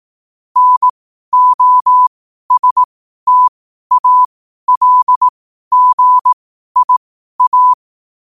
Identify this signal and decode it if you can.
Morse code, heard as a keyed tone, about 18 wpm, spaced as for 12 wpm, 1000 Hz. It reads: NOSTALGIA
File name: qotdfiles / 12wpm